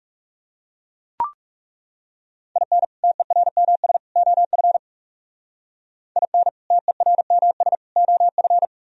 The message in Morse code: IN TERMS OF IN TERMS OF

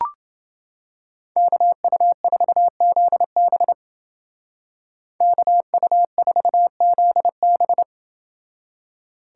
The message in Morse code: KU4ZB KU4ZB

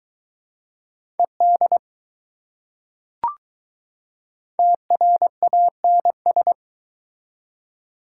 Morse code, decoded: ED TRANS

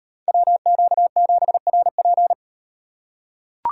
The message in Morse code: WQ7RP

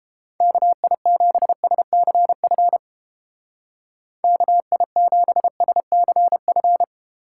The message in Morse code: KI7SCF KI7SCF

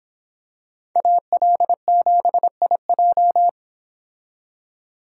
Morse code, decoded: AL7IJ

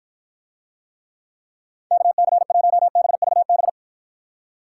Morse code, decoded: KC1BVB

